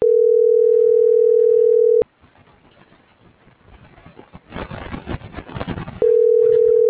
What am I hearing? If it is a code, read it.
Ringback tone (North America), 440 + 480 Hz.